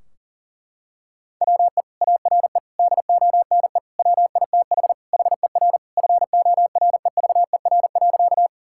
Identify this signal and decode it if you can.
Morse code, heard as a keyed tone, about 40 wpm, 700 Hz. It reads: WE ARE DONE WITH HER FOREVER.
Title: Sentences from Top 1600 Words 40wpm